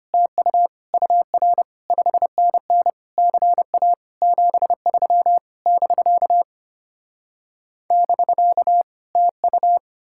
Morse code, heard as a keyed tone, about 33 wpm, 700 Hz. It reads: TU UR 5NN CA 73 <BK> <BK> TU